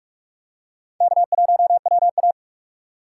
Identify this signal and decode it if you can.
Morse code, heard as a keyed tone, about 46 wpm, 700 Hz. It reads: K1WA
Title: USA Call Signs - 4 Characters - 2x Character Spacing - 45wpm